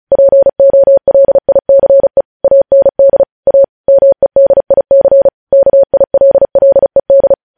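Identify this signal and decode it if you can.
Morse code, heard as a keyed tone, about 36 wpm, 550 Hz. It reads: POLICE AND A MEDIC KILLED
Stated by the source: Morse news headlines sent at 35 words per minute.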